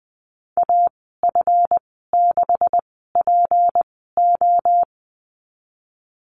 Morse code, heard as a keyed tone, about 20 wpm, spaced as for 14 wpm, 700 Hz. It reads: AF6PO
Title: 186 Lesson 30g - Letter b - Call Signs 20wpm